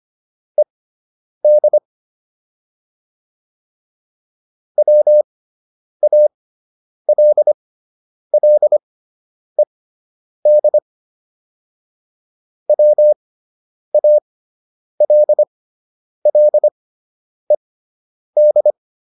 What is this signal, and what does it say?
Morse code, heard as a keyed tone, about 25 wpm, spaced as for 9 wpm, 600 Hz. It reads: ED WALLED WALLED